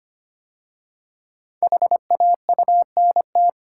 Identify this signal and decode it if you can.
Morse code, heard as a keyed tone, about 25 wpm, 700 Hz. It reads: HAUNT
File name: 5 Letter Words - First 3 Letters Given - 25wpm